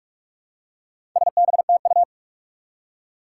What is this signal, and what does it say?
Morse code, heard as a keyed tone, about 46 wpm, 700 Hz. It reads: SBTU
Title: Four Challenging Letter-Number Combinations 45wpm